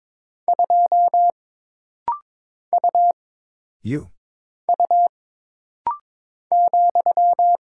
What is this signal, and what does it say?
Morse code, heard as a keyed tone, about 22 wpm, 700 Hz. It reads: 2 U U ,
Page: This rapid-fire timing is geared to push your ICR (Instant Character Recognition) skills.